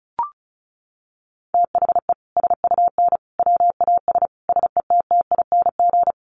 Morse code, heard as a keyed tone, about 34 wpm, 700 Hz. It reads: THE SUN WAS SETTING
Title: Sentences from Top 1300 Words 35wpm